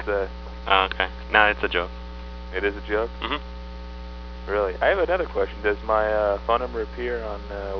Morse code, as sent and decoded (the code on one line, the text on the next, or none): none